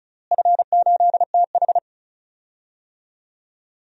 Morse code, heard as a keyed tone, about 36 wpm, 700 Hz. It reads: F8TH
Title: DXCC-100 Call Signs - 4 Characters - Repeated 2x - 35wpm